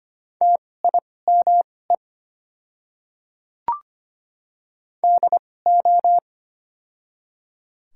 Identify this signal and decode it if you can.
Morse code, heard as a keyed tone, about 25 wpm, spaced as for 18 wpm, 700 Hz. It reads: TIME DO
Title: 2x Character Spacing - Top 100 Words - 25wpm